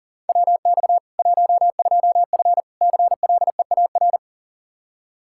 Morse code, heard as a keyed tone, about 40 wpm, 700 Hz. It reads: WX 12F CLEAR
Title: Practice QSO - Exchange 2 - Info - 40wpm